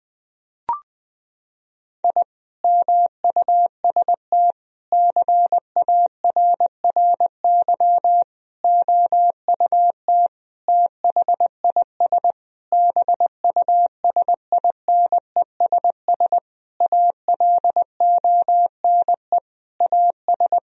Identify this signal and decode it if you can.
Morse code, heard as a keyed tone, about 20 wpm, 700 Hz. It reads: I MUST CARRY OUT THIS BUSINESS ALONE AS
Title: Sentences from Top 1000 Words - Review Part 2 - 20wpm